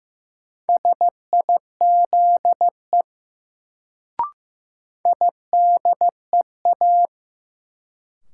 Morse code, heard as a keyed tone, about 15 wpm, 700 Hz. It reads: SIZE IDEA
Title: Sets of 3 Words - New Word Drill - 15wpm